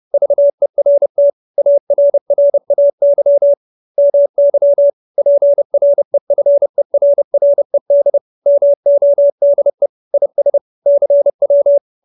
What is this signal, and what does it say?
Morse code, heard as a keyed tone, about 30 wpm, 570 Hz. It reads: VERT ARRAY MY PREFERRED MODE IS CW